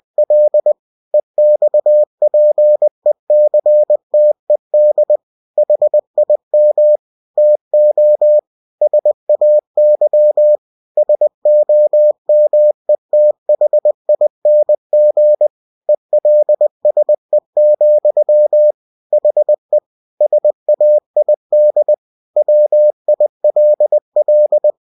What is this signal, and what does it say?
Morse code, heard as a keyed tone, about 20 wpm, 600 Hz. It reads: L EXPECTED HIM TO SAY SOMETHING ELSE, HE SAID WILL
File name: cw-books / winnie-the-pooh / winnie-20wpm-page-0102